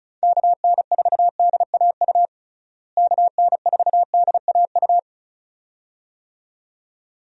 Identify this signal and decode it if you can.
Morse code, heard as a keyed tone, about 34 wpm, 700 Hz. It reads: KN4DAU KN4DAU
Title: USA Call Signs - 6 Characters - Repeated 2x - 35wpm